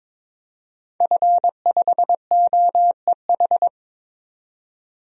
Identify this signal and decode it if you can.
Morse code, heard as a keyed tone, about 22 wpm, 700 Hz. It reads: F5OEH
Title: DXCC-100 Call Signs - 5 Characters - 22wpm